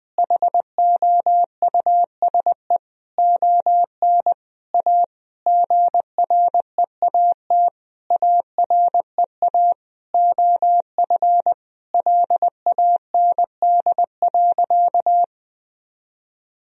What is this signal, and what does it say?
Morse code, heard as a keyed tone, about 19 wpm, 700 Hz. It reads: HOUSE ON A GREAT AREA OF LAND.
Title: Sentences from Top 1000 Words - Review Part 2 - 20wpm